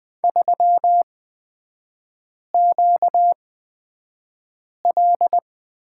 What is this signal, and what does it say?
Morse code, heard as a keyed tone, about 20 wpm, spaced as for 5 wpm, 700 Hz. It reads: 3QL
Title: 243 Lesson 39e - Letter z - Three Characters 20wpm